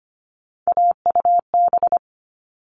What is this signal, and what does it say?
Morse code, heard as a keyed tone, about 25 wpm, 700 Hz. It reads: AUB
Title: All Cut Numbers - 3-Digit 25wpm